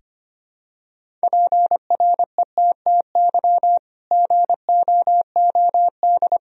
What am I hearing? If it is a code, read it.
Morse code, heard as a keyed tone, about 25 wpm, 700 Hz. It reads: PRETTY GOOD
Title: Sentences from Top 1000 Words - Review Part 1 - 25wpm